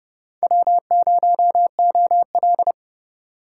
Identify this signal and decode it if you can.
Morse code, heard as a keyed tone, about 29 wpm, 700 Hz. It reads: W0OL